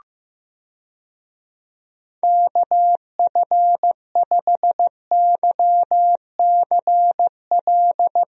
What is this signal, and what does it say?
Morse code, heard as a keyed tone, about 15 wpm, 700 Hz. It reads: KF5YCL
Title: USA Call Signs - 6 Characters - Repeated 3x - 15wpm